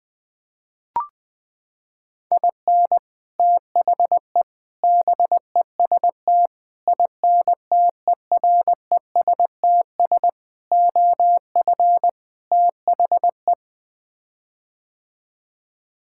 Morse code, heard as a keyed tone, about 20 wpm, 700 Hz. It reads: IN THE BEST INTERESTS OF THE
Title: Sets of 6 Words - 20wpm